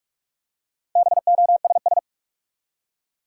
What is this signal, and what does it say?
Morse code, heard as a keyed tone, about 44 wpm, 700 Hz. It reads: BOSS
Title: Four Letter Words - 45wpm